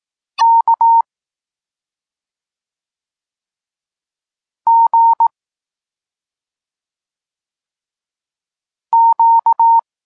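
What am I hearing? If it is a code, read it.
Morse code, heard as a keyed tone, about 18 wpm, 920 Hz. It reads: K G Q
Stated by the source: Polish Morse lesson teaching the letter C, from SP9MOA Niepołomicki Klub Krótkofalowców.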